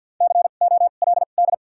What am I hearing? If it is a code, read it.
Morse code, heard as a keyed tone, about 53 wpm, spaced as for 37 wpm, 700 Hz. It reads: COLD
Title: 2x Character Spacing - Top 500 Words - 50wpm